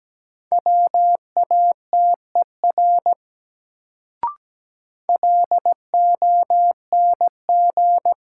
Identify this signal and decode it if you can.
Morse code, heard as a keyed tone, about 17 wpm, 700 Hz. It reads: WATER LONG
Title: Sentences from Top 100 Words - New Word Drill - 17wpm